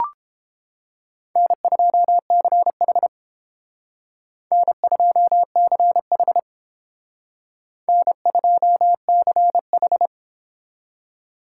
Morse code, heard as a keyed tone, about 28 wpm, 700 Hz. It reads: N2CH N2CH N2CH